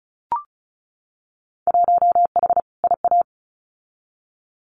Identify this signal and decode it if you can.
Morse code, heard as a keyed tone, about 34 wpm, 700 Hz. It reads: 1H IA
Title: Winter Field Day Exchanges 35wpm